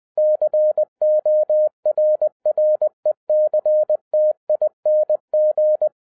Morse code, heard as a keyed tone, about 20 wpm, 610 Hz. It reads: CORRECTING